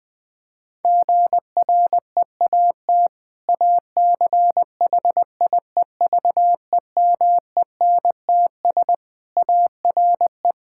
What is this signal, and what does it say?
Morse code, heard as a keyed tone, about 20 wpm, 700 Hz. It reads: GREAT ACHIEVEMENTS ARE
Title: Difficult Sentences with Hints - 20wpm